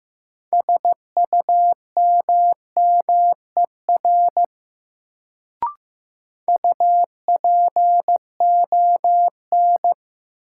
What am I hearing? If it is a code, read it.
Morse code, heard as a keyed tone, about 15 wpm, 700 Hz. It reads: SUMMER UPON
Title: Sentences from Top 1000 Words - Review Part 6 - New Word Drill - 15wpm